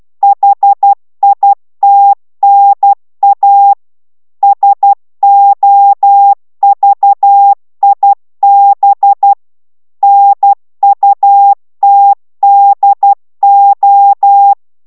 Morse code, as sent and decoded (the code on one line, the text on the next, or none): .... .. - -. .- / ... --- ...- .. -... / -. ..- - -.. ---
HITNA SOVIB NUTDO